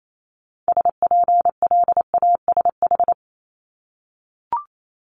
Morse code, heard as a keyed tone, about 28 wpm, 700 Hz. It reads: SPLASH
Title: Multi-letter Phonemes 28wpm